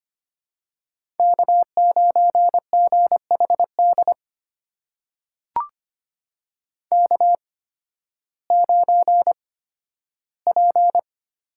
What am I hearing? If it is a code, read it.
Morse code, heard as a keyed tone, about 25 wpm, spaced as for 13 wpm, 700 Hz. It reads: K9GHD K9P
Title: USA Call Signs - 5 Characters - 8x Character Spacing - 25wpm